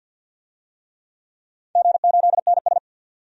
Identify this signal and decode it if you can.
Morse code, heard as a keyed tone, about 48 wpm, 700 Hz. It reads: G8NS
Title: DXCC-100 Call Signs - 4 Characters - 50wpm